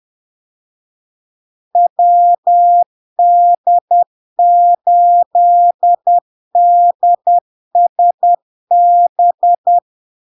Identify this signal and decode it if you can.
Morse code, heard as a keyed tone, about 10 wpm, 700 Hz. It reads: WD8DSB
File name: LARK-FD-2022-Log-in-CW-10wpm-700-Hz